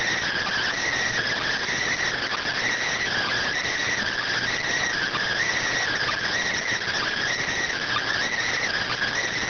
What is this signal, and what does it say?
Emergency Alert System (SAME) tones (header bursts), heard as harsh buzzing bursts. It is heard from 0.2 to 9.2 s.